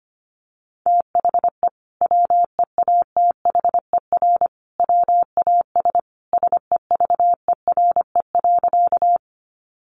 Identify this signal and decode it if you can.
Morse code, heard as a keyed tone, about 25 wpm, 700 Hz. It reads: THE WEATHER WAS SEVERE.
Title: Sentences from Top 1400 Words 25wpm